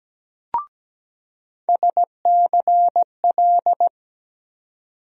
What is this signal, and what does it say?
Morse code, heard as a keyed tone, about 17 wpm, 700 Hz. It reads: SCL
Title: USA Call Sign Postfixes - 3 Letters 17wpm